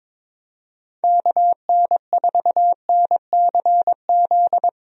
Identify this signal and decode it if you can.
Morse code, heard as a keyed tone, about 22 wpm, 700 Hz. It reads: KN4NCZ
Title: USA Call Signs - 6 Characters - Repeated 3x - 22wpm